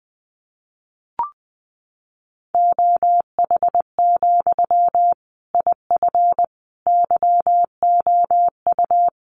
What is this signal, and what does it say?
Morse code, heard as a keyed tone, about 20 wpm, 700 Hz. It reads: OH, IF YOU
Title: Sentences from Top 1100 Words 20wpm